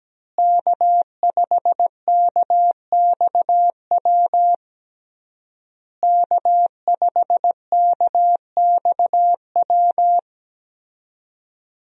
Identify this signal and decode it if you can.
Morse code, heard as a keyed tone, about 17 wpm, 700 Hz. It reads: K5KXW K5KXW
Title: USA Call Signs - 5 Characters - Repeated 3x - 17wpm